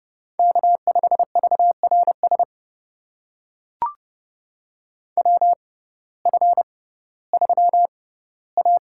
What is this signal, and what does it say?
Morse code, heard as a keyed tone, about 30 wpm, spaced as for 17 wpm, 700 Hz. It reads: K5VRS WF3A
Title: USA Call Signs - 6x Character Spacing - 30wpm